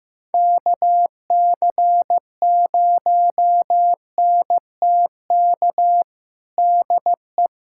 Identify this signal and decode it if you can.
Morse code, heard as a keyed tone, about 15 wpm, 700 Hz. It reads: KC0NTK DE
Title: USA Callsign DE Callsign 15wpm